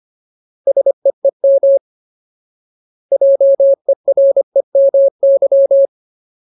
Morse code, heard as a keyed tone, about 25 wpm, 550 Hz. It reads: SEEM JEREMY
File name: Ultimate Word List / 25WPM Ultimate